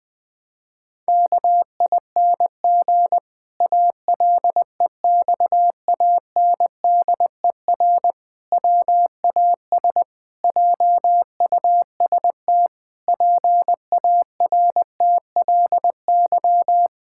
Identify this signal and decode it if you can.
Morse code, heard as a keyed tone, about 20 wpm, 700 Hz. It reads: KING ALEXANDER WAS JUST PARTLY
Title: Pangram Challenge v2 - 20wpm